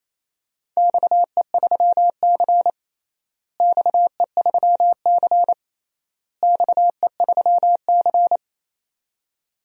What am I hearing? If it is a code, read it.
Morse code, heard as a keyed tone, about 28 wpm, 700 Hz. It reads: XE3C XE3C XE3C
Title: DXCC-100 Call Signs - 4 Characters - Repeated 3x - 28wpm